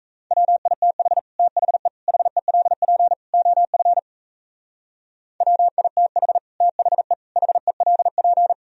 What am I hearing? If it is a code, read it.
Morse code, heard as a keyed tone, about 38 wpm, 700 Hz. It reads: WITH THE HELP OF WITH THE HELP